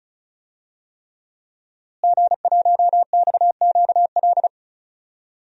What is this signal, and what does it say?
Morse code, heard as a keyed tone, about 34 wpm, 700 Hz. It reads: G1XQL
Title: DXCC-100 Call Signs - 5 Characters - Repeated 3x - 35wpm